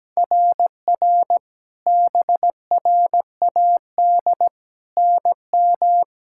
Morse code, heard as a keyed tone, about 17 wpm, 700 Hz. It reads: RR BRAD NM
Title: SOTA QSO Practice 17wpm